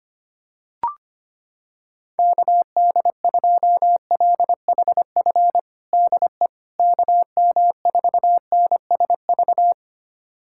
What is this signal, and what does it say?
Morse code, heard as a keyed tone, about 25 wpm, 700 Hz. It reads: KD2LHF DE KM4NSV
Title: USA Callsign DE Callsign - 6 Characters - 25wpm